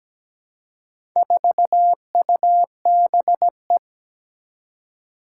Morse code, heard as a keyed tone, about 17 wpm, 700 Hz. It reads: 4UBE
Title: Four Challenging Letter-Number Combinations 17wpm